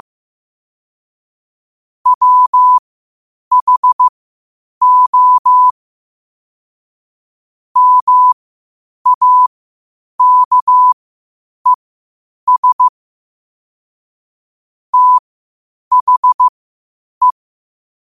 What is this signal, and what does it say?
Morse code, heard as a keyed tone, about 15 wpm, spaced as for 8 wpm, 1000 Hz. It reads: WHO MAKES THE
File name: qotdfiles / 8wpm